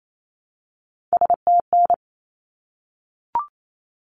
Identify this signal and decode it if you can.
Morse code, heard as a keyed tone, about 28 wpm, 700 Hz. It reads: STN
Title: Abbreviations 28wpm